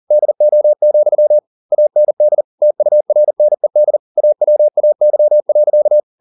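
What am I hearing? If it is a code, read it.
Morse code, heard as a keyed tone, about 40 wpm, 600 Hz. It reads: DO, AND TURNED AWAY.
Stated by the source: Morse code at 40 wpm, practice text, CW med Gnister.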